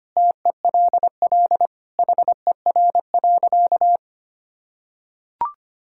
Morse code, heard as a keyed tone, about 25 wpm, 700 Hz. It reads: TELL HER.
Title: Sentences from Top 1000 Words - Review Part 5 - 25wpm